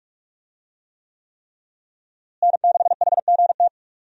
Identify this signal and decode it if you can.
Morse code, heard as a keyed tone, about 46 wpm, 700 Hz. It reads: N6HGT
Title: USA Call Signs - 5 Characters - Repeated 3x - 45wpm